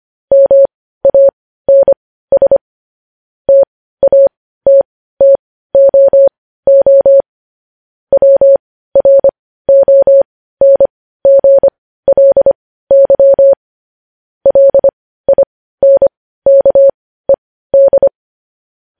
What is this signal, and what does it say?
Morse code, heard as a keyed tone, about 25 wpm, spaced as for 15 wpm, 550 Hz. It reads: MANS TATTOO WRONGLY LINKED
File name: podcast / News headlines-1x-25x15